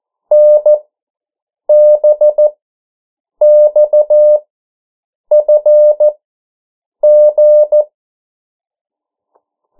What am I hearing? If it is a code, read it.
Morse code, heard as a keyed tone, about 14 wpm, spaced as for 7 wpm, 600 Hz. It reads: NBXFG
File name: morse mp3 / 09 urok 8